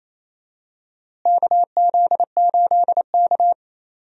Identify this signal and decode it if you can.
Morse code, heard as a keyed tone, about 28 wpm, 700 Hz. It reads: KZ8K
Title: USA Call Signs - 4 Characters - Repeated 3x - 28wpm